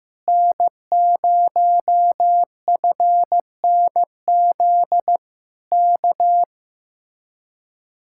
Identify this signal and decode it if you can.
Morse code, heard as a keyed tone, about 15 wpm, 700 Hz. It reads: N0FNZ K
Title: Speed-Racing Common QSO Phrases Encore - 15wpm